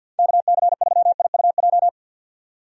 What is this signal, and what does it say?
Morse code, heard as a keyed tone, about 46 wpm, 700 Hz. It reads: KC3IUJ
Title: USA Call Signs - 6 Characters - 50wpm